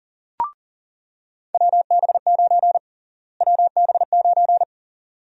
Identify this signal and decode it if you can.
Morse code, heard as a keyed tone, about 40 wpm, 700 Hz. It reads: WB9 WB9
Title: USA Call Sign Prefixes - Repeated 2x - 40wpm